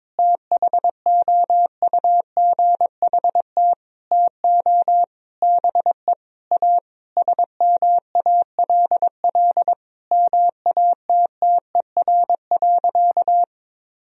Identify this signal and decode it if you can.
Morse code, heard as a keyed tone, about 22 wpm, 700 Hz. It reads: THOUGHT TO BE A SMALL MATTER.
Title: Sentences from Top 1000 Words - Review Part 1 - 22wpm